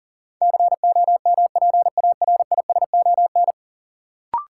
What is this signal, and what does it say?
Morse code, heard as a keyed tone, about 37 wpm, 700 Hz. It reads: COMPARISON
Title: Sentences from Top 1600 Words - New Word Drill - 40wpm